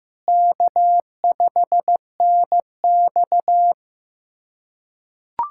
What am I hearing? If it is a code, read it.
Morse code, heard as a keyed tone, about 15 wpm, 700 Hz. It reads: K5NX
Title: USA Callsign DE Callsign - 4 Characters - 15wpm